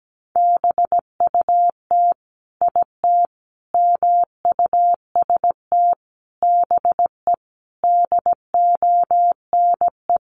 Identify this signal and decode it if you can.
Morse code, heard as a keyed tone, about 17 wpm, 700 Hz. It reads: BUT IT MUST BE DONE
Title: Sentences from Top 500 Words 17wpm